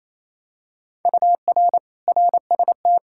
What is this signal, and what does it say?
Morse code, heard as a keyed tone, about 28 wpm, 700 Hz. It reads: UR RST
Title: Practice QSO - Exchange 1 - Beginning - 28wpm